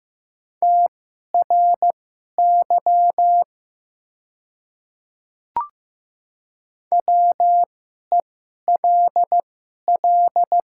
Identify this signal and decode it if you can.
Morse code, heard as a keyed tone, about 15 wpm, spaced as for 11 wpm, 700 Hz. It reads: TRY WELL
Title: 2x Character Spacing - Top 200 Words - 15wpm